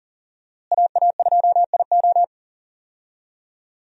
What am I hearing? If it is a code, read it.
Morse code, heard as a keyed tone, about 40 wpm, 700 Hz. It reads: AA2IO
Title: USA Call Signs - 5 Characters - Repeated 3x - 40wpm